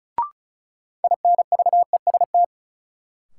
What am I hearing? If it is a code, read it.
Morse code, heard as a keyed tone, about 36 wpm, 700 Hz. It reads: INVEST
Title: Sentences from Top 1100 Words - New Word Drill - 35wpm